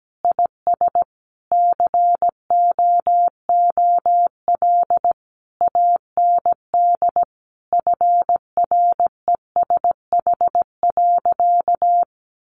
Morse code, heard as a keyed tone, about 17 wpm, 700 Hz. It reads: IS COOL AND FRESH.
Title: Sentences from Top 1200 Words 17wpm